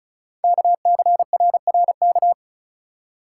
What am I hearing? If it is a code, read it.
Morse code, heard as a keyed tone, about 36 wpm, 690 Hz. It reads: KCRRK
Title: Code Groups - 5 Letter:Numbers - 35wpm